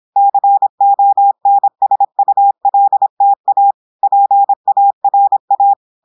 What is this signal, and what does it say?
Morse code, heard as a keyed tone, about 26 wpm, 800 Hz. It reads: CONSULTA PARA
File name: audios / portugais / portugais texte en clair 1 vitesse 26 mots minutes